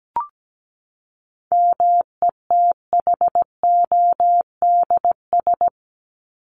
Morse code, heard as a keyed tone, about 17 wpm, 700 Hz. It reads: METHODS
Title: ES:S Suffix - Top 100 Words with Base Words 17wpm